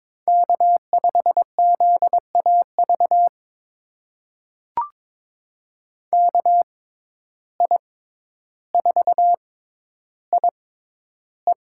Morse code, heard as a keyed tone, about 22 wpm, spaced as for 11 wpm, 700 Hz. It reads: K5ZAV KI4IE
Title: USA Call Signs - 5 Characters - 6x Character Spacing - 22wpm